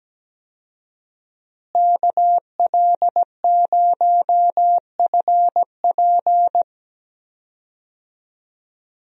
Morse code, heard as a keyed tone, about 17 wpm, 700 Hz. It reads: KL0FP
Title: USA Call Signs - 5 Characters - Repeated 2x - 17wpm